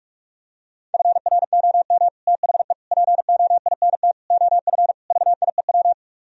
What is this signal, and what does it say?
Morse code, heard as a keyed tone, about 44 wpm, 700 Hz. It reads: FROM THE POINT OF VIEW